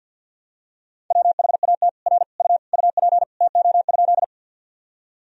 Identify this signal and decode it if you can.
Morse code, heard as a keyed tone, about 48 wpm, 700 Hz. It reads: WHAT R U UP TO?
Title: Sentences from Top 100 Words and Abbreviations 50wpm